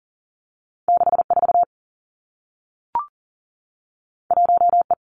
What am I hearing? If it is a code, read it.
Morse code, heard as a keyed tone, about 40 wpm, 700 Hz. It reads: 6 4 1 E
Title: Two Challenging Letter-Number Combinations 40wpm